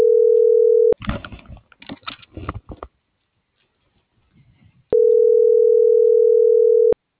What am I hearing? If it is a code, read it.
Ringback tone (North America), 440 + 480 Hz.